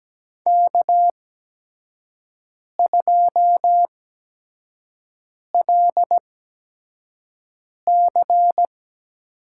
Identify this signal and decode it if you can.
Morse code, heard as a keyed tone, about 17 wpm, spaced as for 5 wpm, 700 Hz. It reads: K2LC
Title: USA Call Signs - 4 Characters - 8x Character Spacing - 17wpm